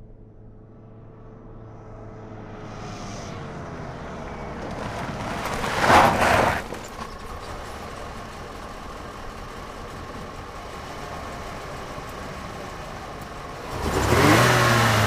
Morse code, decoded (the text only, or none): none